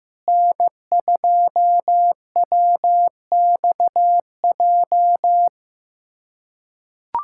Morse code, heard as a keyed tone, about 16 wpm, 700 Hz. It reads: N2WXJ
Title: USA Callsign DE Callsign - 5 Characters - 15wpm